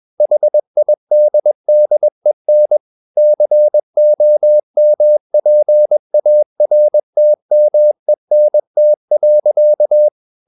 Morse code, heard as a keyed tone, about 21 wpm, 600 Hz. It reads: HIDDEN COMPARTMENT.
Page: CW med Gnister